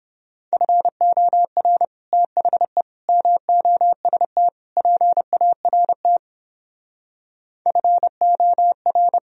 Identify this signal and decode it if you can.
Morse code, heard as a keyed tone, about 29 wpm, 700 Hz. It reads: FOR THE MOST PART FOR